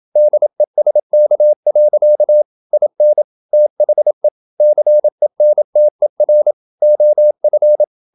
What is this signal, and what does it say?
Morse code, heard as a keyed tone, about 27 wpm, 600 Hz. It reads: DESK. IN THE CENTER OF
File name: Never 0027wpm